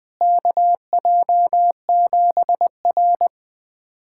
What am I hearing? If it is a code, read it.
Morse code, heard as a keyed tone, about 20 wpm, 700 Hz. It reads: KJ7R
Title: USA Callsign DE Callsign - 4 Characters - 20wpm